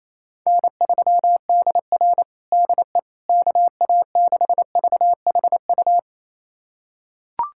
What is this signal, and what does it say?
Morse code, heard as a keyed tone, about 28 wpm, 700 Hz. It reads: N3DR DE KA6VHU
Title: USA Callsign DE Callsign 28wpm